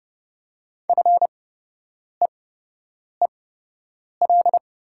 Morse code, heard as a keyed tone, about 30 wpm, spaced as for 8 wpm, 700 Hz. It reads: FEEL
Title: 8x Character Spacing - Top 500 Words - 30wpm